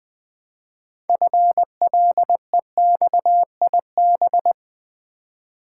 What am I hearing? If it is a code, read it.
Morse code, heard as a keyed tone, about 20 wpm, 700 Hz. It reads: FLEXIB
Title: Build A Word - 2 Letters At A Time - 8-Letter Words - 20wpm